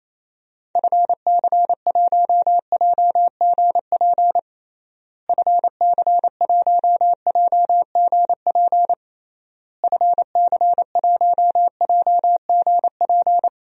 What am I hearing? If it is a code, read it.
Morse code, heard as a keyed tone, about 28 wpm, 700 Hz. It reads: FC1JGP FC1JGP FC1JGP
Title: DXCC-100 Call Signs - 6 Characters - Repeated 3x - 28wpm